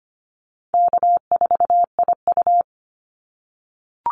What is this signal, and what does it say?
Morse code, heard as a keyed tone, about 25 wpm, 700 Hz. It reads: K4IU
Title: Easy Call Signs - 25wpm